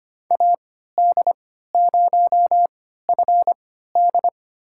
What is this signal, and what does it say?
Morse code, heard as a keyed tone, about 25 wpm, spaced as for 14 wpm, 700 Hz. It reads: AD0FD
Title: USA Call Signs - 5 Characters - 3x Character Spacing - 25wpm